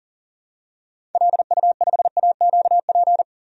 Morse code, heard as a keyed tone, about 37 wpm, 700 Hz. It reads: LU5AQP
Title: DXCC-100 Call Signs - 6 Characters - 40wpm